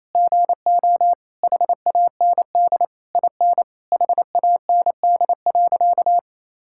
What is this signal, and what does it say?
Morse code, heard as a keyed tone, about 27 wpm, 700 Hz. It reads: GO HAND IN HAND.
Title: Sentences from Top 1000 Words - Review Part 5 - 28wpm